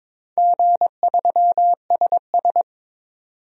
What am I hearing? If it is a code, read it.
Morse code, heard as a keyed tone, about 22 wpm, 700 Hz. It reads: G3SS
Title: DXCC-100 Call Signs - 4 Characters - 22wpm